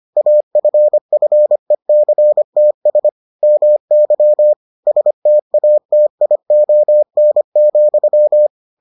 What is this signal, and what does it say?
Morse code, heard as a keyed tone, about 25 wpm, 600 Hz. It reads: AFFECTS MY STATION,